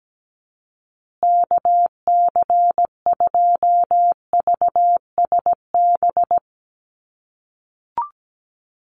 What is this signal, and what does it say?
Morse code, heard as a keyed tone, about 17 wpm, 700 Hz. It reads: KC2VSB
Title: USA Call Signs Repeated 3x 17wpm